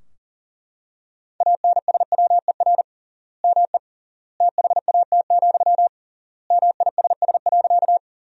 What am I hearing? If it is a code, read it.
Morse code, heard as a keyed tone, about 37 wpm, 700 Hz. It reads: ANSWER ME THAT, MISS.
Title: 3x Word-Spacing - Sentences from Top 500 Words - 40wpm